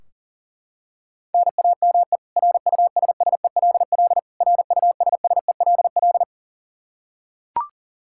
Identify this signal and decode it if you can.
Morse code, heard as a keyed tone, about 40 wpm, 700 Hz. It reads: NAME RUSSELL RUSSELL
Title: K1USN SST QSO Practice 40wpm